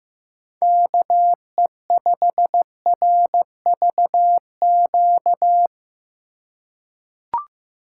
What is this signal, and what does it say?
Morse code, heard as a keyed tone, about 15 wpm, 700 Hz. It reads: KE5RVQ
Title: USA Call Signs - 6 Characters - 2x Character Spacing - 15wpm